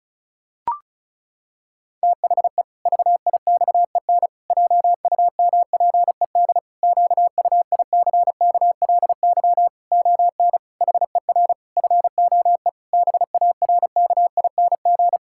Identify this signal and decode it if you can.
Morse code, heard as a keyed tone, about 36 wpm, 700 Hz. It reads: THE VIXEN JUMPED QUICKLY ON HER FOE BARKING
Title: Pangram Challenge v2 - 35wpm